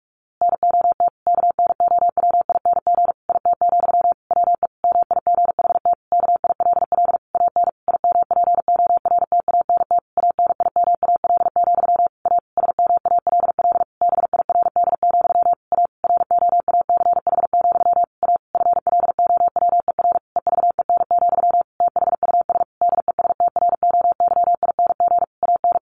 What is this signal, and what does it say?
Morse code, heard as a keyed tone, about 44 wpm, 700 Hz. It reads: NOT KNOWING IT, WE MIGHT KILL AN IMPORTANT ANIMAL, A SMALL BIRD, A ROACH, A FLOWER EVEN, THUS DESTROYING AN
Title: A Sound Of Thunder 45wpm